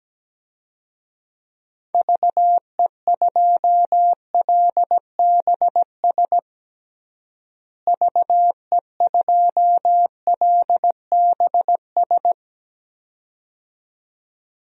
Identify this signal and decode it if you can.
Morse code, heard as a keyed tone, about 17 wpm, 700 Hz. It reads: VE2LBS VE2LBS
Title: DXCC-100 Call Signs - 6 Characters - Repeated 2x - 17wpm